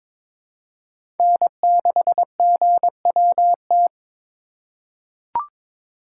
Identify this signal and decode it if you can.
Morse code, heard as a keyed tone, about 21 wpm, 700 Hz. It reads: N6GWT
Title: USA Call Signs - 5 Characters - 2x Character Spacing - 22wpm